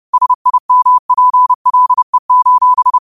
Morse code, heard as a keyed tone, about 30 wpm, 1000 Hz. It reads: SIMPLE:
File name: qotdfiles / 30wpm